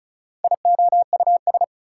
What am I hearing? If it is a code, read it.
Morse code, heard as a keyed tone, about 36 wpm, 700 Hz. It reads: IOUS
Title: Prefixes and Suffixes 35wpm